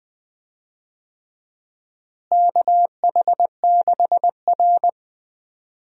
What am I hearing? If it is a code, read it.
Morse code, heard as a keyed tone, about 20 wpm, 700 Hz. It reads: KH6R
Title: USA Call Signs - 4 Characters - Repeated 3x - 20wpm